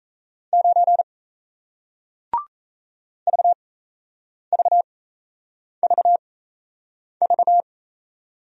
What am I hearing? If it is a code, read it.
Morse code, heard as a keyed tone, about 37 wpm, spaced as for 8 wpm, 700 Hz. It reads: 9 VVVV